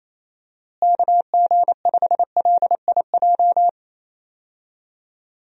Morse code, heard as a keyed tone, about 28 wpm, 700 Hz. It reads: KG5LIJ
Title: USA Call Signs - 6 Characters - Repeated 3x - 28wpm